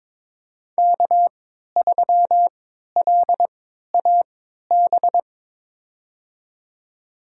Morse code, heard as a keyed tone, about 22 wpm, spaced as for 13 wpm, 700 Hz. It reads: K3LAB
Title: USA Call Signs - 5 Characters - 3x Character Spacing - 22wpm